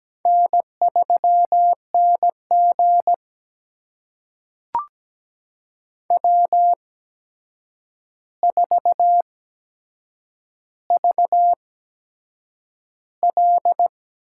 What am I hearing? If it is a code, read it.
Morse code, heard as a keyed tone, about 17 wpm, spaced as for 7 wpm, 700 Hz. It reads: N3NG W4VL
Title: USA Call Signs - 4 Characters - 8x Character Spacing - 17wpm